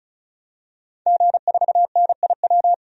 Morse code, heard as a keyed tone, about 34 wpm, 700 Hz. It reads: G4NIW
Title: DXCC-100 Call Signs - 5 Characters - 35wpm